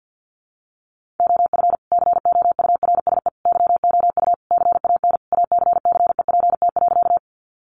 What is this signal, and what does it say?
Morse code, heard as a keyed tone, about 50 wpm, 700 Hz. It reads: OF COURSE YOU CAN ACCEPT.
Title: Sentences from Top 1300 Words 50wpm